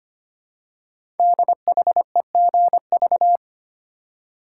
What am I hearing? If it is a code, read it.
Morse code, heard as a keyed tone, about 25 wpm, 700 Hz. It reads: DHEGV
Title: Code Groups - 5 Letters - 25wpm